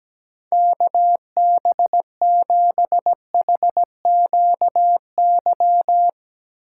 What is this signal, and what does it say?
Morse code, heard as a keyed tone, about 17 wpm, 700 Hz. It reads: KB7HQY
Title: USA Call Signs - 6 Characters - 2x Character Spacing - 17wpm